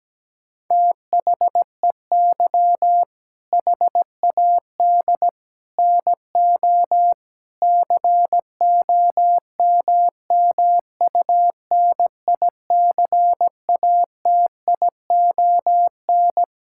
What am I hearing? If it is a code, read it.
Morse code, heard as a keyed tone, about 17 wpm, 700 Hz. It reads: THEY HAD NO COMMUNICATION
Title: Sentences from Top 1000 Words - Review Part 6 - 17wpm